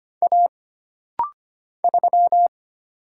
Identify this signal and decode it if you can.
Morse code, heard as a keyed tone, about 25 wpm, 700 Hz. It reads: A 3
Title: Single Letter-Number - Rapid-Fire 25wpm